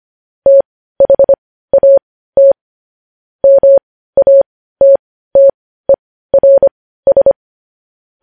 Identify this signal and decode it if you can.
Morse code, heard as a keyed tone, about 25 wpm, spaced as for 15 wpm, 550 Hz. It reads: THAT MATTERS
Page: News Headlines in Morse Code at 15 WPM